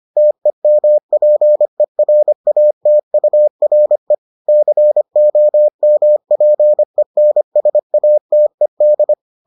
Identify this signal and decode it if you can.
Morse code, heard as a keyed tone, about 25 wpm, 600 Hz. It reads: TEMPERATURE COMPENSATED